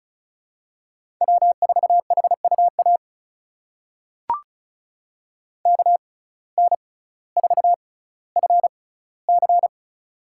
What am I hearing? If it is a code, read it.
Morse code, heard as a keyed tone, about 36 wpm, spaced as for 18 wpm, 700 Hz. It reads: W4HUA KN4FC
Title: USA Call Signs - 5 Characters - 6x Character Spacing - 35wpm